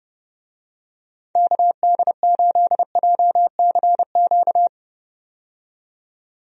Morse code, heard as a keyed tone, about 30 wpm, 700 Hz. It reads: KD8JCQ
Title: USA Call Signs - 6 Characters - 30wpm